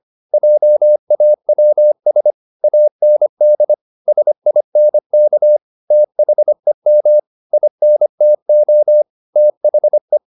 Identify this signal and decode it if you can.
Morse code, heard as a keyed tone, about 25 wpm, 600 Hz. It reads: JAWS AND SINK THEM INTO THE